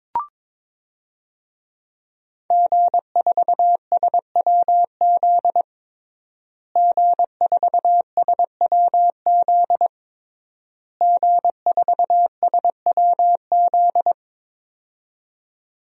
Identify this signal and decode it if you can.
Morse code, heard as a keyed tone, about 22 wpm, 700 Hz. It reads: G4SWZ G4SWZ G4SWZ
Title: DXCC-100 Call Signs - 5 Characters - Repeated 3x - 22wpm